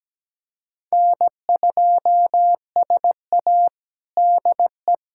Morse code, heard as a keyed tone, about 17 wpm, 700 Hz. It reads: N2SA DE
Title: USA Callsign DE Callsign 17wpm